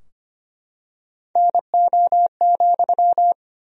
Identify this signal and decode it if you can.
Morse code, heard as a keyed tone, about 25 wpm, 700 Hz. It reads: NO,
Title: 2x Word-Spacing - Sentences from Top 200 Words - 25wpm